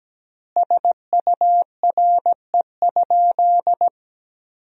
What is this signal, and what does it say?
Morse code, heard as a keyed tone, about 17 wpm, 700 Hz. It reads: SURE?
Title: 4x Word-Spacing - Sentences from Top 300 Words - 17wpm